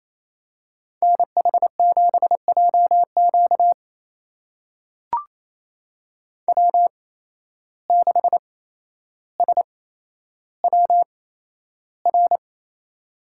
Morse code, heard as a keyed tone, about 28 wpm, spaced as for 12 wpm, 700 Hz. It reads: NH7JQ W6SWR
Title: USA Call Signs - 5 Characters - 8x Character Spacing - 28wpm